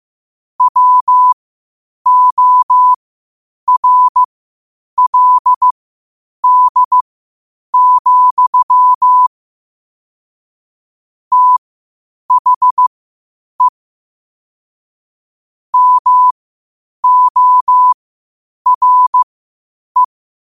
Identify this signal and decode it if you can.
Morse code, heard as a keyed tone, about 15 wpm, spaced as for 8 wpm, 1000 Hz. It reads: WORLD, THE MORE
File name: qotdfiles / 8wpm